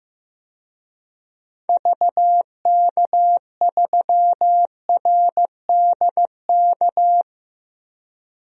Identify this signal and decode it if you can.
Morse code, heard as a keyed tone, about 14 wpm, 700 Hz. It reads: VK3RDK
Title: DXCC-100 Call Signs - 6 Characters - Repeated 2x - 15wpm